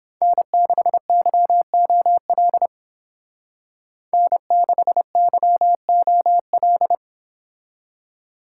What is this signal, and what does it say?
Morse code, heard as a keyed tone, about 29 wpm, 700 Hz. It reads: N6YOL N6YOL